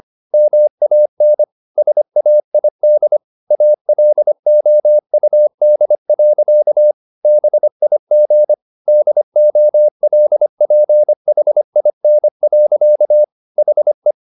Morse code, heard as a keyed tone, about 25 wpm, 600 Hz. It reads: MAN SAID ALOUD. BIG DOLPHIN. HE
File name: the old man and the sea part 1 25wpm